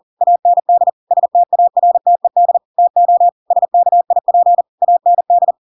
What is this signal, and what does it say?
Morse code, heard as a keyed tone, about 40 wpm, 700 Hz. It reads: AND STARTED TO SKIP AND
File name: Pinocchio / Pinocchio ch8 40 40 ik1hge